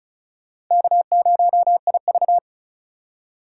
Morse code, heard as a keyed tone, about 47 wpm, spaced as for 41 wpm, 700 Hz. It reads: K0IV
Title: Speed Racing - USA 1x2 Call Signs 35wpm